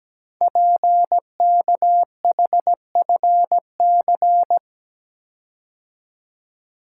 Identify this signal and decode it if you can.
Morse code, heard as a keyed tone, about 17 wpm, 700 Hz. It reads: PKHFC
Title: Code Groups - 5 Letters - 17wpm